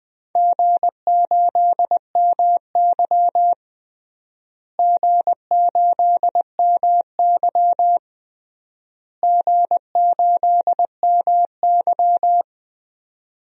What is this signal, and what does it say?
Morse code, heard as a keyed tone, about 20 wpm, 700 Hz. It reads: G8MY G8MY G8MY
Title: DXCC-100 Call Signs - 4 Characters - Repeated 3x - 20wpm